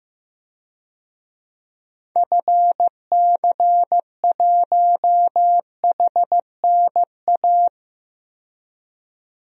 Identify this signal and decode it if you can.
Morse code, heard as a keyed tone, about 16 wpm, 700 Hz. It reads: FC1HNA
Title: DXCC-100 Call Signs - 6 Characters - 15wpm